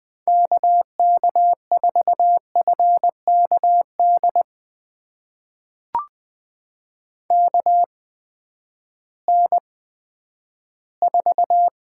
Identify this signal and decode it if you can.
Morse code, heard as a keyed tone, about 20 wpm, spaced as for 11 wpm, 700 Hz. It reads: KK4FKD KN4
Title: USA Call Signs - 6 Characters - 8x Character Spacing - 20wpm